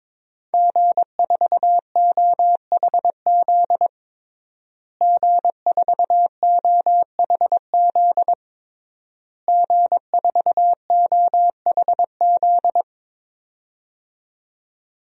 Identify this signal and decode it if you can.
Morse code, heard as a keyed tone, about 21 wpm, 700 Hz. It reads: G4OHZ G4OHZ G4OHZ
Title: DXCC-100 Call Signs - 5 Characters - Repeated 3x - 22wpm